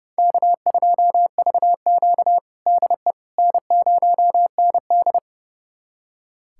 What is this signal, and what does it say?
Morse code, heard as a keyed tone, about 30 wpm, 700 Hz. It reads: K2VQ DE N0ND
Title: USA Callsign DE Callsign - 4 Characters - 30wpm